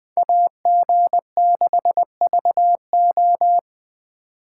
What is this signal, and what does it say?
Morse code, heard as a keyed tone, about 20 wpm, 700 Hz. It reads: AG6VO
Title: Speed Racing - USA 2x2 Call Signs 17wpm